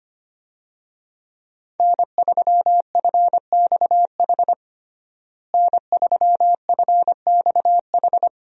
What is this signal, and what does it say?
Morse code, heard as a keyed tone, about 25 wpm, 700 Hz. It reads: N3FXH N3FXH
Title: USA Call Signs - 5 Characters - Repeated 2x - 25wpm